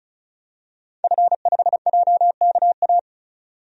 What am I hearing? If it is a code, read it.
Morse code, heard as a keyed tone, about 36 wpm, 700 Hz. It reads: F5JKA
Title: DXCC-100 Call Signs - 5 Characters - 35wpm